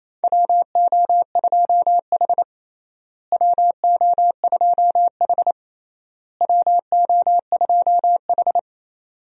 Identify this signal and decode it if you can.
Morse code, heard as a keyed tone, about 28 wpm, 700 Hz. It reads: WO2H WO2H WO2H
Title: USA Call Signs - 4 Characters - Repeated 3x - 28wpm